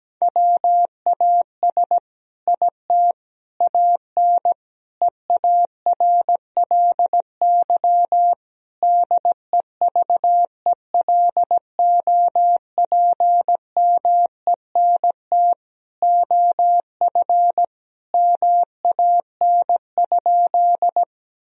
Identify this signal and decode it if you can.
Morse code, heard as a keyed tone, about 17 wpm, 700 Hz. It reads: WAS IT AN EARLY DEVELOPMENT OF MAN?
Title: Sentences from Top 1000 Words - Review Part 4 - 17wpm